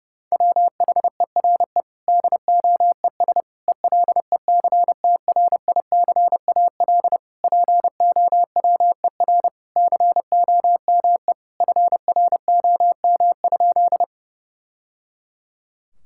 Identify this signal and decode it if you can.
Morse code, heard as a keyed tone, about 30 wpm, 700 Hz. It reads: WHERE DOES ELECTRICAL POWER COME FROM?
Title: Sentences from Top 1000 Words - Review Part 6 - 30wpm